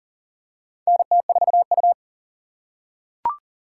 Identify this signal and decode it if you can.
Morse code, heard as a keyed tone, about 42 wpm, 690 Hz. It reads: NT4U
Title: USA Call Signs - 4 Characters - 2x Character Spacing - 40wpm